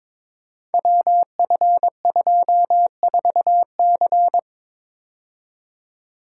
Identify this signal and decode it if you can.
Morse code, heard as a keyed tone, about 22 wpm, 700 Hz. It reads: WF24C
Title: Code Groups - 5 Letter:Numbers - 22wpm